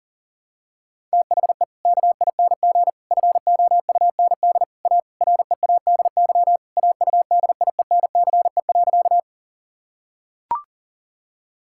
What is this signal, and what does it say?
Morse code, heard as a keyed tone, about 37 wpm, 700 Hz. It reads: THE KING FOUND A READY AUDIENCE.
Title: Sentences from Top 1000 Words - Review Part 5 - 40wpm